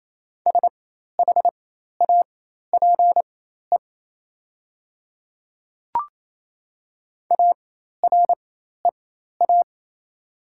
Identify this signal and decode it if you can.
Morse code, heard as a keyed tone, about 28 wpm, spaced as for 13 wpm, 700 Hz. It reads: SHAPE AREA
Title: 4x Character Spacing - Top 500 Words - 28wpm